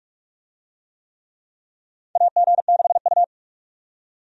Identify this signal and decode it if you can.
Morse code, heard as a keyed tone, about 43 wpm, 700 Hz. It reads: AG6U
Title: USA Call Signs - 4 Characters - 45wpm